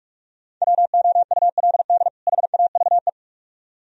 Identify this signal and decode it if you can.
Morse code, heard as a keyed tone, about 45 wpm, 700 Hz. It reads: WOULD HAVE